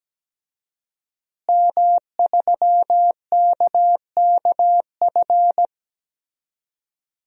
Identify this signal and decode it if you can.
Morse code, heard as a keyed tone, about 17 wpm, 700 Hz. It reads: M3KKF
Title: DXCC-100 Call Signs - 5 Characters - Repeated 3x - 17wpm